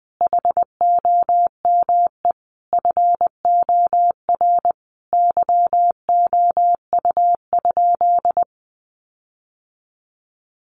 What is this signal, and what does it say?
Morse code, heard as a keyed tone, about 20 wpm, 700 Hz. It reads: HOME FOR YOU?
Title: Sentences from Top 1400 Words 20wpm